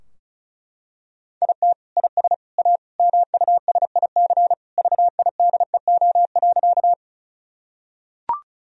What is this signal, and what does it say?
Morse code, heard as a keyed tone, about 36 wpm, 700 Hz. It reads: IT IS A MUSIC VIDEO.
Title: Sentences from Top 1000 Words - Review Part 4 - 35wpm